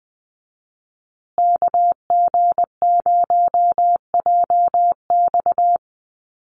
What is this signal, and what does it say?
Morse code, heard as a keyed tone, about 20 wpm, 700 Hz. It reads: KG0JX
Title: USA Call Signs Repeated 3x 20wpm